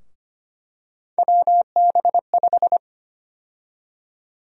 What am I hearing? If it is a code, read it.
Morse code, heard as a keyed tone, about 25 wpm, 700 Hz. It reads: WB5
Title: USA Call Sign Prefixes - 25wpm